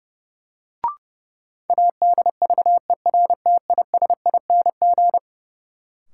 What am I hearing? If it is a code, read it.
Morse code, heard as a keyed tone, about 32 wpm, 700 Hz. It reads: ADVERTISING
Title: Sentences from Top 1000 Words - Review Part 6 - New Word Drill - 30wpm